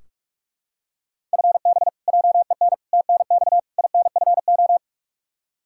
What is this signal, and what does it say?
Morse code, heard as a keyed tone, about 46 wpm, 700 Hz. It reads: FB JEN TNX INFO
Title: Practice QSO - Exchange 3 - More Info - 45wpm